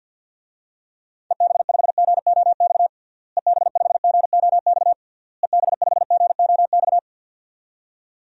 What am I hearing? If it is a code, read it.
Morse code, heard as a keyed tone, about 51 wpm, 700 Hz. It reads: EB5GOX EB5GOX EB5GOX
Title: DXCC-100 Call Signs - 6 Characters - Repeated 3x - 50wpm